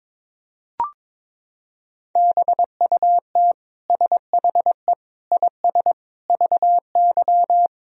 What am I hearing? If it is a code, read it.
Morse code, heard as a keyed tone, about 21 wpm, 700 Hz. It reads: BUT SHE IS VY
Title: Sentences from Top 400 Words and Abbreviations 22wpm